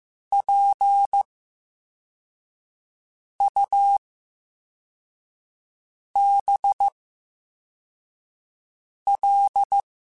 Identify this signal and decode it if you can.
Morse code, heard as a keyed tone, about 15 wpm, spaced as for 4 wpm, 800 Hz. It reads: PUBL
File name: Special INGs / Special INGs-mp3s / Slow ing @13